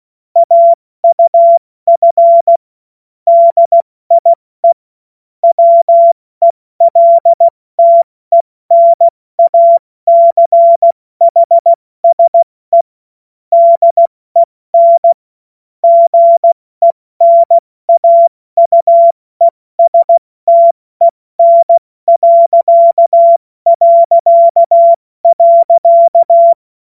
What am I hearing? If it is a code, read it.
Morse code, heard as a keyed tone, about 16 wpm, 680 Hz. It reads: AUF DIE WELTENACHSE DEN GENAUESTEN...
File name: Morsekurs / Witze / Witz 10.Oktober 2023 70 F80